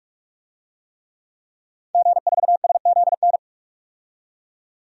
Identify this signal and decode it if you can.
Morse code, heard as a keyed tone, about 46 wpm, 700 Hz. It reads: G4SZN
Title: DXCC-100 Call Signs - 5 Characters - 45wpm